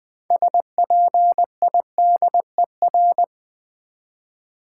Morse code, heard as a keyed tone, about 20 wpm, 700 Hz. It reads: SPIDER
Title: Build A Word - 3 Letters At A Time - 6-Letter Words - 20wpm